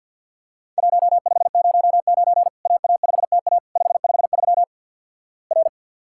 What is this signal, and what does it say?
Morse code, heard as a keyed tone, about 50 wpm, 700 Hz. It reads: 1509 AA5TA 553 R
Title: Worked All Europe DX Contest - QTC Exchange with Repeats - 50wpm